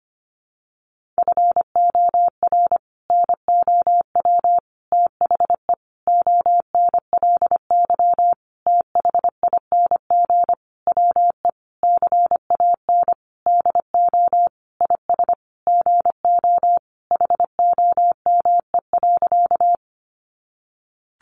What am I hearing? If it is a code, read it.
Morse code, heard as a keyed tone, about 25 wpm, 700 Hz. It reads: FOR NOW THE ONLY THING WE CAN DO IS GO HOME.
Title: Sentences from Top 200 Words 25wpm